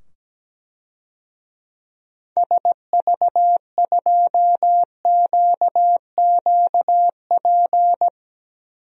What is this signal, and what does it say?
Morse code, heard as a keyed tone, about 17 wpm, 700 Hz. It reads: SV2QQP
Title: DXCC-100 Call Signs - 6 Characters - 17wpm